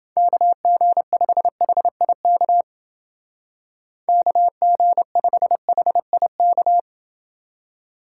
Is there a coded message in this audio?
Yes: Morse code from 0.2 to 6.8 s, heard as a keyed tone, about 28 wpm, 700 Hz. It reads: KG5HIK KG5HIK